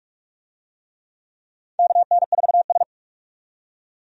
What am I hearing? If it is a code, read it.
Morse code, heard as a keyed tone, about 46 wpm, 700 Hz. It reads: KN4S
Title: USA Call Signs - 4 Characters - 45wpm